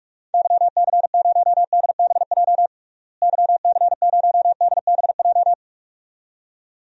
Morse code, heard as a keyed tone, about 47 wpm, 700 Hz. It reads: YC0DBJ YC0DBJ
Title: DXCC-100 Call Signs - 6 Characters - Repeated 2x - 45wpm